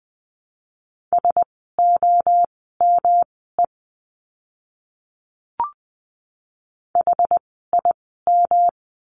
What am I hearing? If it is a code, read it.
Morse code, heard as a keyed tone, about 20 wpm, spaced as for 15 wpm, 700 Hz. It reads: SOME HIM
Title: 115 Lesson 19f - Letter w - Words 20wpm